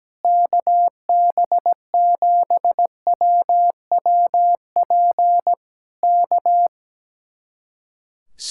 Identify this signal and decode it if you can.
Morse code, heard as a keyed tone, about 17 wpm, 700 Hz. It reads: KB7WWP K
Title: Speed-Racing Common QSO Phrases - 17wpm